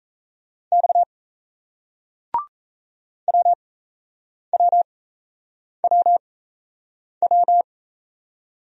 Morse code, heard as a keyed tone, about 37 wpm, spaced as for 8 wpm, 700 Hz. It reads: X WWWW